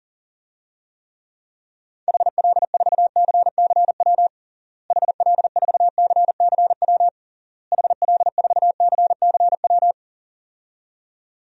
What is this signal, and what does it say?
Morse code, heard as a keyed tone, about 37 wpm, 700 Hz. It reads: HL4CCW HL4CCW HL4CCW
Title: DXCC-100 Call Signs - 6 Characters - Repeated 3x - 40wpm